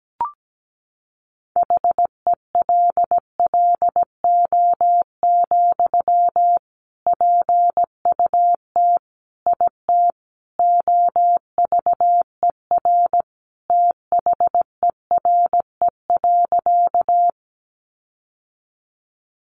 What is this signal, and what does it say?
Morse code, heard as a keyed tone, about 17 wpm, 700 Hz. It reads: HELLO, PUT IT OVER THERE.
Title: Sentences from Top 1100 Words 17wpm